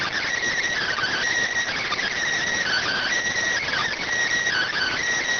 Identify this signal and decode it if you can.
Emergency Alert System (SAME) tones, heard as harsh buzzing bursts. It is heard from 0.8 to 5.0 s.